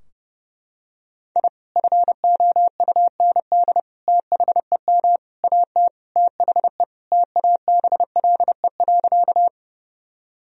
Morse code, heard as a keyed tone, about 30 wpm, 700 Hz. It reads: I FOUND THEM AT THE TABLE.
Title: Sentences from Top 1000 Words - Review Part 2 - 30wpm